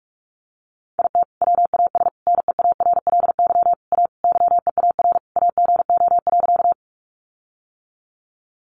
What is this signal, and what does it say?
Morse code, heard as a keyed tone, about 44 wpm, 700 Hz. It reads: IT WAS NEARLY A YEAR AGO.
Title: Sentences from Top 1300 Words 45wpm